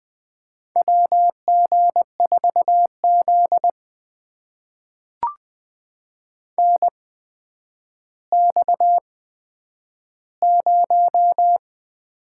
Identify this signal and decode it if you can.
Morse code, heard as a keyed tone, about 20 wpm, spaced as for 10 wpm, 700 Hz. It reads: WG4Z NX0
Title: USA Call Signs - 4 Characters - 8x Character Spacing - 20wpm